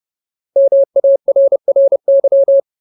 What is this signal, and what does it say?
Morse code, heard as a keyed tone, about 30 wpm, 550 Hz. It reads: MARRY
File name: Ultimate Word List / 30WPM Ultimate